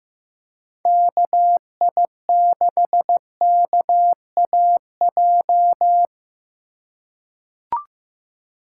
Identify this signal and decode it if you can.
Morse code, heard as a keyed tone, about 15 wpm, 700 Hz. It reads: KI6KAJ
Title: USA Call Signs - 6 Characters - 2x Character Spacing - 15wpm